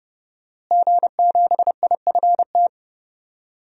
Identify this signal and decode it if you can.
Morse code, heard as a keyed tone, about 30 wpm, 700 Hz. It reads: G7IFT
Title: DXCC-100 Call Signs - 5 Characters - Repeated 3x - 30wpm